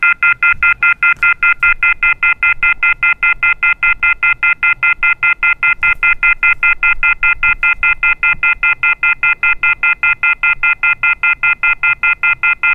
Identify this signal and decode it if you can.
Off-hook warning tone, 1400 + 2060 + 2450 + 2600 Hz, with noise.
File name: grab